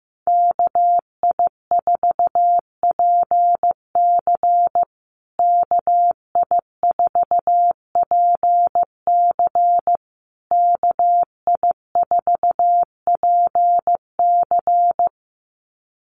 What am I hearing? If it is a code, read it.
Morse code, heard as a keyed tone, about 15 wpm, 700 Hz. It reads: KI4PC KI4PC KI4PC
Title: USA Call Signs Repeated 3x 15wpm